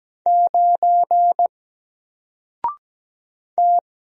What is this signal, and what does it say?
Morse code, heard as a keyed tone, about 17 wpm, 700 Hz. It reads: 9 T
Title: Single Letter-Number 17wpm